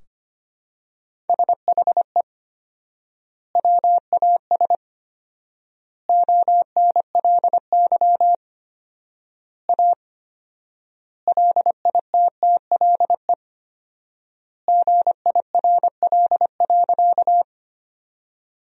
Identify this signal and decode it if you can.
Morse code, heard as a keyed tone, about 25 wpm, 700 Hz. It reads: SHE WAS ONLY A LITTLE GIRL.
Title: 4x Word-Spacing - Sentences from Top 400 Words - 25wpm